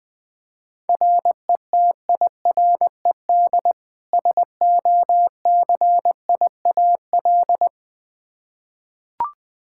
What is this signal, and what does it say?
Morse code, heard as a keyed tone, about 20 wpm, 700 Hz. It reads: RETIRED SOCIAL
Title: Practice QSO - Exchange 3 - More Info - 20wpm